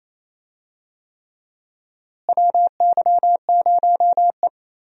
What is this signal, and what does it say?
Morse code, heard as a keyed tone, about 28 wpm, 700 Hz. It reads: WY0E
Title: USA Call Signs - 4 Characters - Repeated 3x - 28wpm